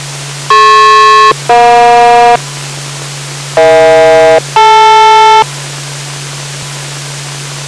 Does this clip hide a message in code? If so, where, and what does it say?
Yes: aircraft SELCAL from 0.5 to 5.4 s, heard as two pairs of tones. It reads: DP-EJ EH-DL